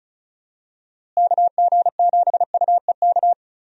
Morse code, heard as a keyed tone, about 36 wpm, 700 Hz. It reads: KG7UEK
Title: USA Call Signs - 6 Characters - Repeated 3x - 35wpm